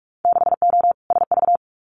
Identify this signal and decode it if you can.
Morse code, heard as a keyed tone, about 46 wpm, 700 Hz. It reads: 6O SV
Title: Winter Field Day Exchanges Encore 45wpm